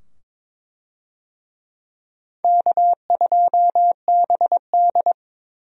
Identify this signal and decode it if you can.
Morse code, heard as a keyed tone, about 22 wpm, 700 Hz. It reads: K2BD
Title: USA Call Signs - 4 Characters - 22wpm